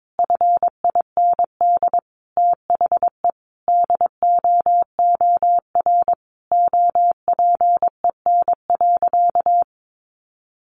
Morse code, heard as a keyed tone, about 22 wpm, 700 Hz. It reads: FIND THE DOOR OPEN.
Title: Sentences from Top 1000 Words 22wpm